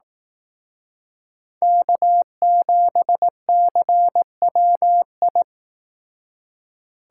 Morse code, heard as a keyed tone, about 18 wpm, 700 Hz. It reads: K7CWI
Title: Speed Racing - USA 1x3 Call Signs 15wpm